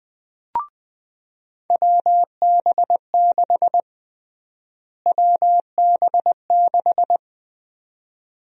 Morse code, heard as a keyed tone, about 20 wpm, 700 Hz. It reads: WB6 WB6
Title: USA Call Sign Prefixes - Repeated 2x - 20wpm